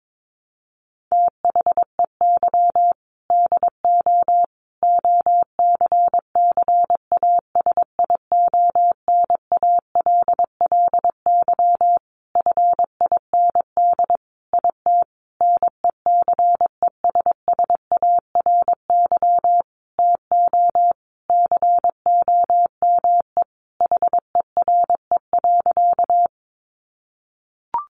Morse code, heard as a keyed tone, about 22 wpm, 700 Hz. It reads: THEY DO OCCASIONALLY FIND IT NECESSARY TO COME HERE.
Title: Sentences from Top 1300 Words 22wpm